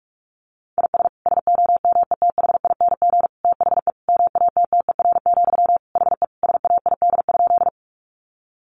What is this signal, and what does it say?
Morse code, heard as a keyed tone, about 44 wpm, 700 Hz. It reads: IS SOMETHING THE MATTER, HE SAID?
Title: Sentences from Top 700 Words 45wpm